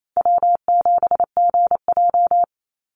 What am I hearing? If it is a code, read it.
Morse code, heard as a keyed tone, about 28 wpm, 700 Hz. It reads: W7GJ
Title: NA SOTA Chaser Callsigns 28wpm